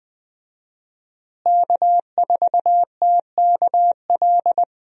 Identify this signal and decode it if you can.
Morse code, heard as a keyed tone, about 20 wpm, 700 Hz. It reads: K4TKL
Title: USA Call Signs - 5 Characters - Repeated 2x - 20wpm